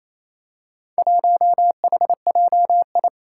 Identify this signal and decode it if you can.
Morse code, heard as a keyed tone, about 27 wpm, 700 Hz. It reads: 1HJI
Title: Four Challenging Letter-Number Combinations 28wpm